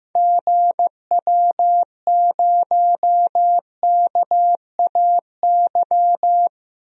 Morse code, heard as a keyed tone, about 15 wpm, 700 Hz. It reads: GW0KAY
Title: DXCC-100 Call Signs - 6 Characters - 15wpm